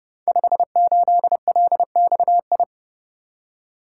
Morse code, heard as a keyed tone, about 30 wpm, 700 Hz. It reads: 58LXI
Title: Code Groups - 5 Letter:Numbers - 30wpm